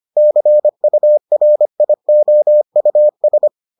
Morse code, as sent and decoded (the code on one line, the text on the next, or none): -.-. ..- .-. .. --- ..- ...
CURIOUS